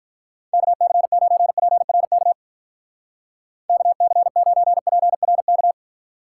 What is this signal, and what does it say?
Morse code, heard as a keyed tone, about 43 wpm, 700 Hz. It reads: KC9PRK KC9PRK
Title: Speed Racing - USA 2x3 Call Signs 35wpm